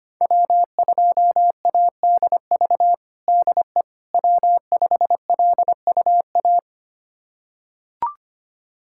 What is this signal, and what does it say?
Morse code, heard as a keyed tone, about 25 wpm, 700 Hz. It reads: W2ADV DE W5LUA
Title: USA Callsign DE Callsign - 5 Characters - 25wpm